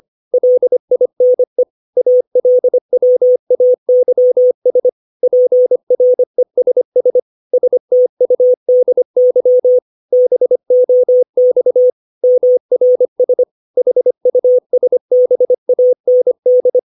Morse code, heard as a keyed tone, about 25 wpm, 500 Hz. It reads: LINE ALWAYS PRESS STUDY BOX MRS HUSBAND
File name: 632 words / 632 Words 25WPM